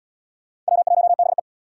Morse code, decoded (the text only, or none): W1BE